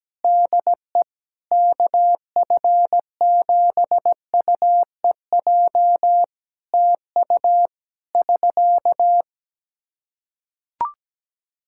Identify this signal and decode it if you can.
Morse code, heard as a keyed tone, about 17 wpm, 700 Hz. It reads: DE KF7UEJ TU <SK>
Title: K1USN SST QSO Practice 17wpm